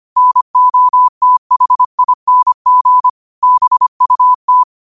Morse code, heard as a keyed tone, about 25 wpm, 1000 Hz. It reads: NOTHING BUT
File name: qotdfiles / 25wpm